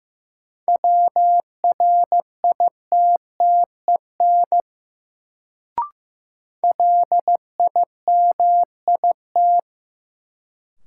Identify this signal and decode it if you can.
Morse code, heard as a keyed tone, about 15 wpm, 700 Hz. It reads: WRITTEN LIMIT
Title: Sentences from Top 1500 Words - New Word Drill - 15wpm